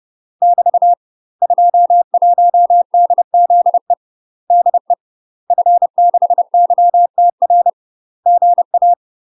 Morse code, heard as a keyed tone, about 30 wpm, 700 Hz. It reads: <BT> 21DZE DE F6YTR GA
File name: UK MASTER QSOs TRAINING / 30WPM / UK MASTER QSOs 0029